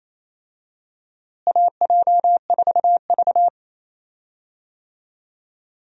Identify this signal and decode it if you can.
Morse code, heard as a keyed tone, about 27 wpm, 700 Hz. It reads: AJ4V
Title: USA Call Signs - 4 Characters - Repeated 3x - 28wpm